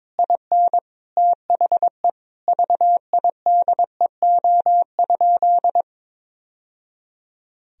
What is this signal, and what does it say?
Morse code, heard as a keyed tone, about 22 wpm, 700 Hz. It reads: IN THE VIDEO?
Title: Sentences from Top 1000 Words - Review Part 4 - 22wpm